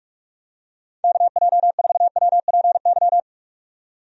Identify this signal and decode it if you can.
Morse code, heard as a keyed tone, about 46 wpm, 700 Hz. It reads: KJ4WPY
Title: USA Call Signs - 6 Characters - 45wpm